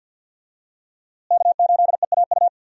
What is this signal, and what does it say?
Morse code, heard as a keyed tone, about 51 wpm, 700 Hz. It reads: K8EAU
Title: USA Call Signs - 5 Characters - Repeated 2x - 50wpm